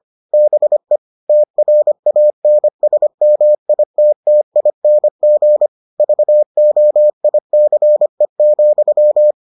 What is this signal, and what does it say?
Morse code, heard as a keyed tone, about 25 wpm, 600 Hz. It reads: BE TRANSMITTING VOICE,